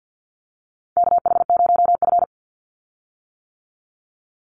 Morse code, heard as a keyed tone, about 50 wpm, 700 Hz. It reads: KH0F
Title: USA Call Signs 50wpm